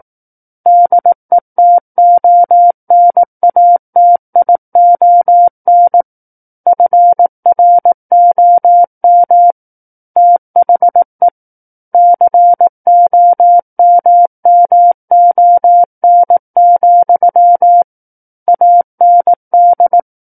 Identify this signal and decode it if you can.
Morse code, heard as a keyed tone, about 18 wpm, 700 Hz. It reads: DETONATION FROM THE COMMON, AND
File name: 18 WPM